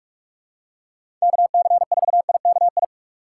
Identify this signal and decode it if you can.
Morse code, heard as a keyed tone, about 43 wpm, 700 Hz. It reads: KC4IKI
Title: USA Call Signs - 2x Character Spacing - 45wpm